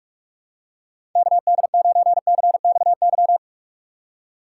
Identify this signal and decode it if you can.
Morse code, heard as a keyed tone, about 46 wpm, 700 Hz. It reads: KD9CXY
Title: USA Call Signs - 2x Character Spacing - 45wpm